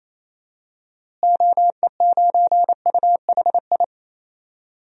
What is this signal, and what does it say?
Morse code, heard as a keyed tone, about 28 wpm, 700 Hz. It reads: OE9UHI
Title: DXCC-100 Call Signs - 6 Characters - Repeated 2x - 28wpm